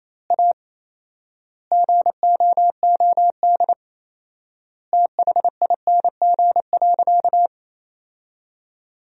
Morse code, heard as a keyed tone, about 28 wpm, 700 Hz. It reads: A GOOD THING.
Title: 4x Word-Spacing - Sentences from Top 300 Words - 28wpm